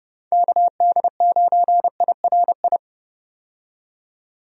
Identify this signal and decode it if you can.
Morse code, heard as a keyed tone, about 30 wpm, 700 Hz. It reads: KD9IRI
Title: USA Call Signs - 6 Characters - 30wpm